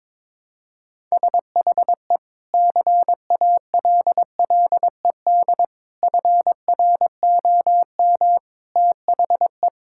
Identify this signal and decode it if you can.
Morse code, heard as a keyed tone, about 22 wpm, 700 Hz. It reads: SHE CALLED FROM THE
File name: practice / 22wpm / Sentences with 2-Word Phrases - 22wpm